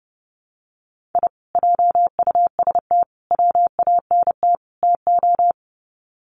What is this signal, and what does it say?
Morse code, heard as a keyed tone, about 30 wpm, 700 Hz. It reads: I JUST WANT TO
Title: Sets of 4 Words 30wpm